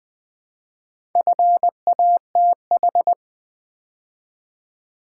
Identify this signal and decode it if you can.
Morse code, heard as a keyed tone, about 20 wpm, 690 Hz. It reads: FATH
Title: Build A Word - 2 Letters At A Time - 6-Letter Words - 20wpm